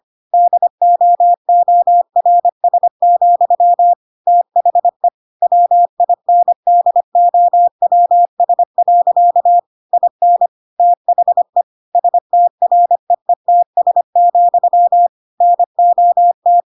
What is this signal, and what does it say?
Morse code, heard as a keyed tone, about 25 wpm, 700 Hz. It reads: DOORS, THE WINDOWS. IN THE STREETS, NOT